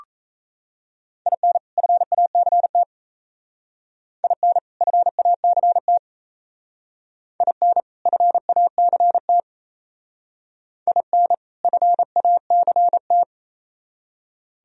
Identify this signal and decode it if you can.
Morse code, heard as a keyed tone, about 35 wpm, 700 Hz. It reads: IN FACT IN FACT IN FACT IN FACT
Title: Speed-Racing - Sets of 2 Words 28wpm